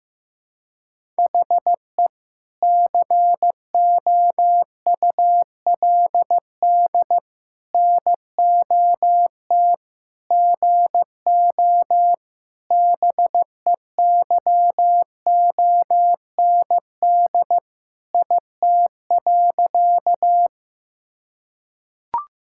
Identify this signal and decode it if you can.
Morse code, heard as a keyed tone, about 16 wpm, 700 Hz. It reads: HE COULD NOT GO BEYOND IT.
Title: Sentences from Top 1000 Words - Review Part 6 - 15wpm